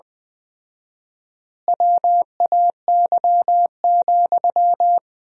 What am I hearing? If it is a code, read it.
Morse code, heard as a keyed tone, about 21 wpm, 700 Hz. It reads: WAY,
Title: 4x Word-Spacing - Sentences from Top 400 Words - 20wpm